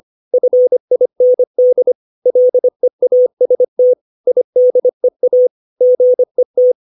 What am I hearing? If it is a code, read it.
Morse code, heard as a keyed tone, about 25 wpm, 500 Hz. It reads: FIND LEAST IDEA GET
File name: 632 words / 632 Words 25WPM